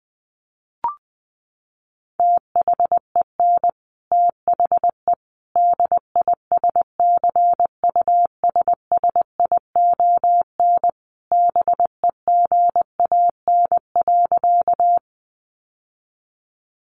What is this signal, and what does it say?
Morse code, heard as a keyed tone, about 19 wpm, 700 Hz. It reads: THEN THE DISCUSSION BEGAN.
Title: Sentences from Top 1100 Words 20wpm